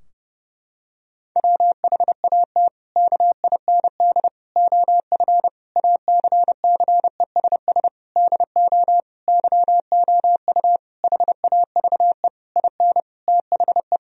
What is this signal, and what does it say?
Morse code, heard as a keyed tone, about 32 wpm, 700 Hz. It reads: WHAT KIND OF ACCESS DO YOU HAVE IN THE
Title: Sentences from Top 1000 Words - Review Part 4 - 30wpm